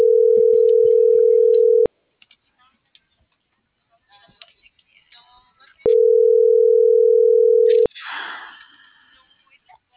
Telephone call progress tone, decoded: ringback tone

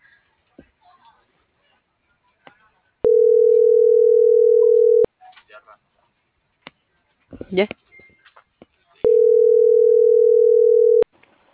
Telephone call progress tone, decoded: ringback tone

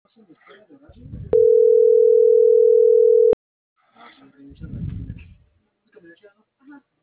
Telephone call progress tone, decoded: ringback tone

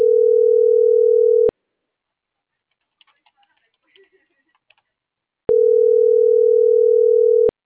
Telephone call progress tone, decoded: ringback tone

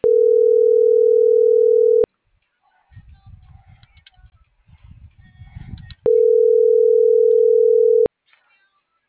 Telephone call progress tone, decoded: ringback tone

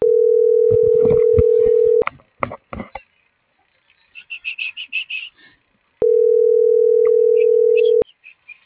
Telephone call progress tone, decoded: ringback tone

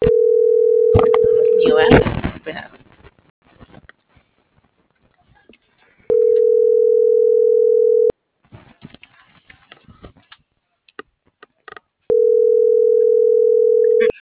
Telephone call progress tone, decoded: ringback tone

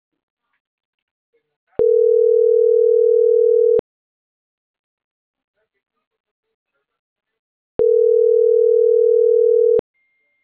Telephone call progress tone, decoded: ringback tone

